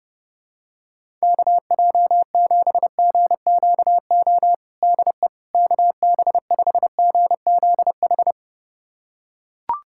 Morse code, heard as a keyed tone, about 28 wpm, 700 Hz. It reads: KJ7GQO DE KB5GZH
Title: USA Callsign DE Callsign - 6 Characters - 30wpm